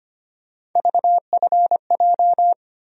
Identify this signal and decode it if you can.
Morse code, heard as a keyed tone, about 25 wpm, 700 Hz. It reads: VFJ
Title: USA Call Sign Postfixes - 3 Letters 25wpm